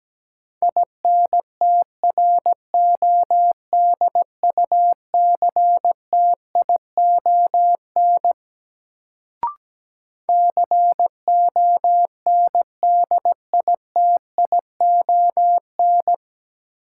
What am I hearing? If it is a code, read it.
Morse code, heard as a keyed tone, about 17 wpm, 700 Hz. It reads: INTRODUCTION CONDITION
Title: Sentences from Top 1000 Words - New Word Drill - 17wpm